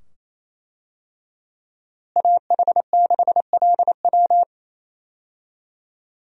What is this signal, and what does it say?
Morse code, heard as a keyed tone, about 27 wpm, 700 Hz. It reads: AH6LW
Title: USA Call Signs - 5 Characters - Repeated 2x - 28wpm